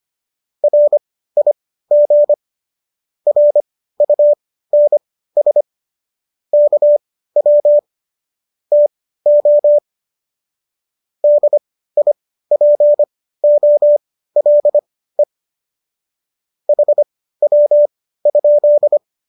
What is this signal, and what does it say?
Morse code, heard as a keyed tone, about 25 wpm, spaced as for 15 wpm, 590 Hz. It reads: RIG RUNS KW TO DIPOLE HW?
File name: qso211 15